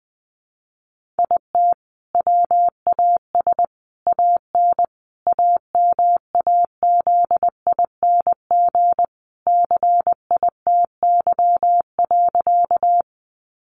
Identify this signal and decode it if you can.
Morse code, heard as a keyed tone, about 20 wpm, 700 Hz. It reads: IT WAS AN AMAZING CITY.
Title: Sentences from Top 1500 Words 20wpm